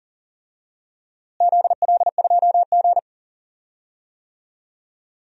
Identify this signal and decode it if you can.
Morse code, heard as a keyed tone, about 38 wpm, 700 Hz. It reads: ZL2G
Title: DXCC-100 Call Signs - 4 Characters - Repeated 3x - 40wpm